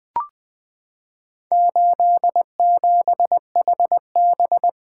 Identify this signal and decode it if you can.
Morse code, heard as a keyed tone, about 20 wpm, 700 Hz. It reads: 87HB
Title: Four Letter-Number Combinations - 20wpm